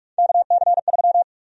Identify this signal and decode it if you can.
Morse code, heard as a keyed tone, about 43 wpm, 700 Hz. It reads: KC3
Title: USA Call Sign Prefixes - 45wpm